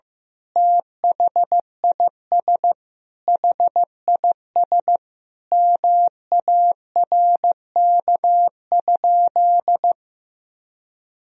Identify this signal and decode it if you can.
Morse code, heard as a keyed tone, about 15 wpm, 700 Hz. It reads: THIS HIS MARK?
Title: Sentences from Top 1000 Words - Review Part 1 - 15wpm